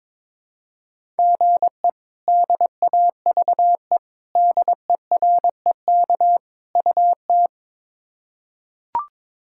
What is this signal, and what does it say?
Morse code, heard as a keyed tone, about 21 wpm, 700 Hz. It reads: GE DAVE DEREK UT
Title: K1USN SST QSO Practice - 22wpm